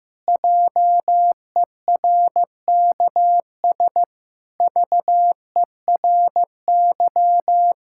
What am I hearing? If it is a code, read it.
Morse code, heard as a keyed tone, about 16 wpm, 700 Hz. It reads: JERKS VERY
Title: Pangram Challenge v2 - 15wpm